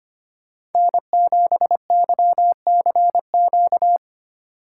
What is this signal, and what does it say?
Morse code, heard as a keyed tone, about 30 wpm, 700 Hz. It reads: N7YCQ
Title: Speed Racing - USA 1x3 Call Signs 25wpm